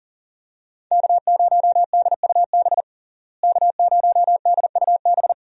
Morse code, heard as a keyed tone, about 40 wpm, 700 Hz. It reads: K0DUB K0DUB
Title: USA Call Signs - 5 Characters - Repeated 2x - 40wpm